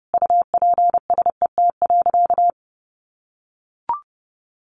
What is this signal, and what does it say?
Morse code, heard as a keyed tone, about 28 wpm, 700 Hz. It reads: UPSET.
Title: Sentences from Top 1200 Words 30wpm